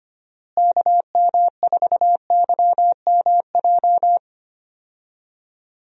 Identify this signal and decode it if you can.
Morse code, heard as a keyed tone, about 25 wpm, 700 Hz. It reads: KM4YMJ
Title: USA Call Signs - 6 Characters - Repeated 2x - 25wpm